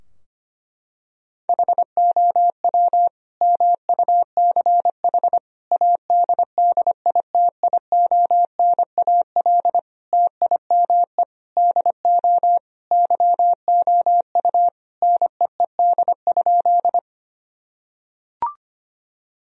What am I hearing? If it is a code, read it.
Morse code, heard as a keyed tone, about 25 wpm, 700 Hz. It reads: HOW MUCH ADDITIONAL TIME DO YOU NEED?
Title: Sentences from Top 1000 Words - Review Part 4 - 25wpm